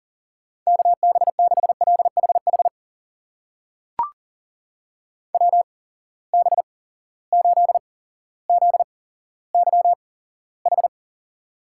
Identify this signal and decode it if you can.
Morse code, heard as a keyed tone, about 40 wpm, spaced as for 17 wpm, 700 Hz. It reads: KB6LHH WB8ZYH
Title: USA Call Signs - 6 Characters - 8x Character Spacing - 40wpm